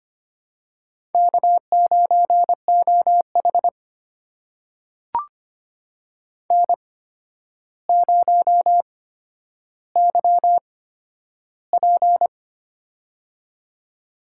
Morse code, heard as a keyed tone, about 25 wpm, spaced as for 11 wpm, 700 Hz. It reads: K9OH N0YP
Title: USA Call Signs - 4 Characters - 8x Character Spacing - 25wpm